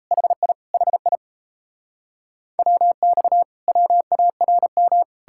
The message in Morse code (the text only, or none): HI HI WX WARM